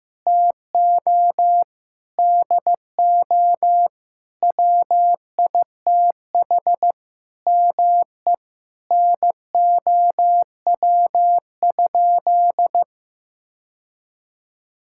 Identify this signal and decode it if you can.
Morse code, heard as a keyed tone, about 15 wpm, 700 Hz. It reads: TO DO WITH ME NOW?
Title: Sentences from Top 1600 Words 15wpm